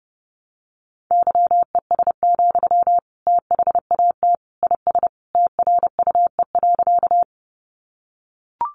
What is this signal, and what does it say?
Morse code, heard as a keyed tone, about 30 wpm, 700 Hz. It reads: YES, THAT IS TRUE.
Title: Sentences from Top 500 Words 30wpm